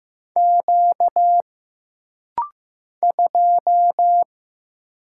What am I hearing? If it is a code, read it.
Morse code, heard as a keyed tone, about 15 wpm, 700 Hz. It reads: Q 2
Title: Single Letter-Number Alt - Rapid-Fire 15wpm